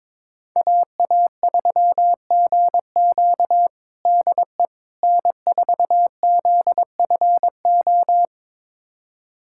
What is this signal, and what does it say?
Morse code, heard as a keyed tone, about 22 wpm, 700 Hz. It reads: AA3GQ DE N4ZFO
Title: USA Callsign DE Callsign - 5 Characters - 22wpm